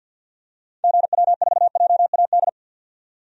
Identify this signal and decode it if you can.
Morse code, heard as a keyed tone, about 48 wpm, 700 Hz. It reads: GW4JAD
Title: DXCC-100 Call Signs - 6 Characters - 50wpm